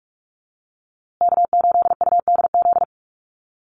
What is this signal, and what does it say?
Morse code, heard as a keyed tone, about 44 wpm, 700 Hz. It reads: K8UDZ
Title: USA Call Signs Repeated 3x 45wpm